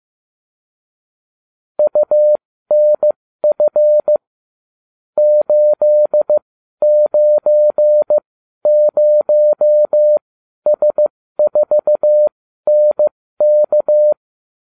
Morse code, heard as a keyed tone, about 15 wpm, spaced as for 12 wpm, 600 Hz. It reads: UNF 890S4NK